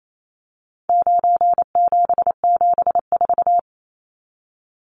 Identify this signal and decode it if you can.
Morse code, heard as a keyed tone, about 28 wpm, 700 Hz. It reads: 9774
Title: Numbers - 4-Digit 28wpm